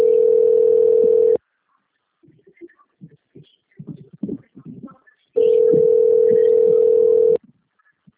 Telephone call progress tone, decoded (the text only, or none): ringback tone